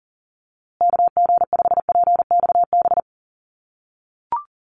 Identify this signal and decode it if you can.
Morse code, heard as a keyed tone, about 40 wpm, 700 Hz. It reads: KG5PXB
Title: USA Call Signs Repeated 2x 40wpm